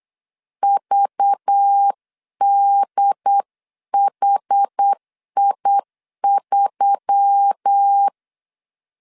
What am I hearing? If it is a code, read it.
Morse code, heard as a keyed tone, about 8 wpm, 800 Hz. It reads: VDHI3
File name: LEZIONI / 6 PKX0 / 6 PKX 8wpm 3041